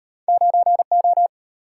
Morse code, heard as a keyed tone, about 34 wpm, 700 Hz. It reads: 9O